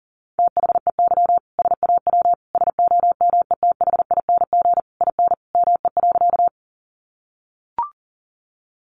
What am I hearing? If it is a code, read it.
Morse code, heard as a keyed tone, about 37 wpm, 700 Hz. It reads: THEY SAW SOMETHING IN ME.
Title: Sentences from Top 600 Words 40wpm